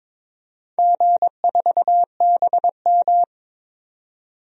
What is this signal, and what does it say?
Morse code, heard as a keyed tone, about 22 wpm, 700 Hz. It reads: G4BM
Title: DXCC-100 Call Signs - 4 Characters - 22wpm